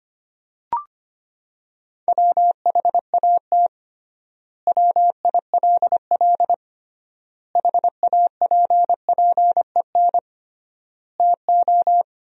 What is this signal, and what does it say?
Morse code, heard as a keyed tone, about 25 wpm, 700 Hz. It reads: WHAT WILL HAPPEN TO
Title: 3x Word-Spacing - Sentences from Top 400 Words - 25wpm